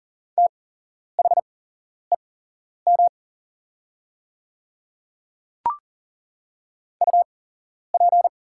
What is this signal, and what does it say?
Morse code, heard as a keyed tone, about 40 wpm, spaced as for 11 wpm, 690 Hz. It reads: THEM UP
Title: 8x Character Spacing - Top 100 Words - 40wpm